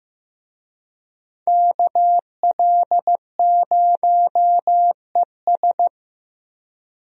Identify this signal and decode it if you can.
Morse code, heard as a keyed tone, about 15 wpm, 700 Hz. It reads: KL0ES
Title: USA Call Signs - 5 Characters - Repeated 2x - 15wpm